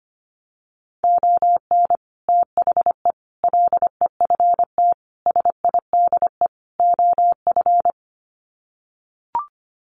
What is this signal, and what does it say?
Morse code, heard as a keyed tone, about 25 wpm, 700 Hz. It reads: ON THE LEFT SIDE OF
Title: Sets of 5 Words Encore2 25wpm